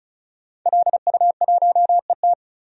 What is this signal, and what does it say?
Morse code, heard as a keyed tone, about 34 wpm, 700 Hz. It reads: LU1ET
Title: DXCC-100 Call Signs - 5 Characters - 35wpm